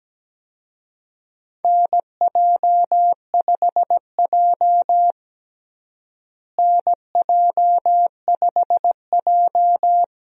Morse code, heard as a keyed tone, about 17 wpm, 700 Hz. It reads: NJ5J NJ5J
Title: USA Call Signs - 4 Characters - Repeated 3x - 17wpm